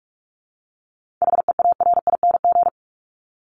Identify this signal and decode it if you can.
Morse code, heard as a keyed tone, about 44 wpm, 700 Hz. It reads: HEARING
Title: Top 1751-2000 Words 45wpm